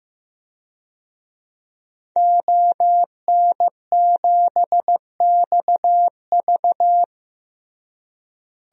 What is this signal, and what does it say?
Morse code, heard as a keyed tone, about 15 wpm, 700 Hz. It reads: ON7XV
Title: DXCC-100 Call Signs - 5 Characters - Repeated 2x - 15wpm